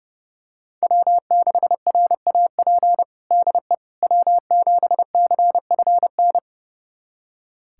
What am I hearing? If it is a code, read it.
Morse code, heard as a keyed tone, about 30 wpm, 700 Hz. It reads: W6RAP DE W7CFN
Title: USA Callsign DE Callsign - 5 Characters - 30wpm